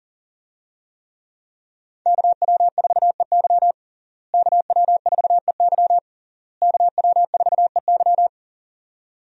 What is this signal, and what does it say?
Morse code, heard as a keyed tone, about 37 wpm, 690 Hz. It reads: KW4EY KW4EY KW4EY
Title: USA Call Signs - 5 Characters - Repeated 3x - 40wpm